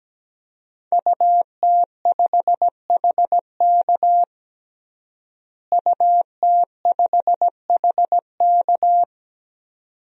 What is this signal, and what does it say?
Morse code, heard as a keyed tone, about 17 wpm, 700 Hz. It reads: UT5HK UT5HK
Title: DXCC-100 Call Signs - 5 Characters - Repeated 2x - 17wpm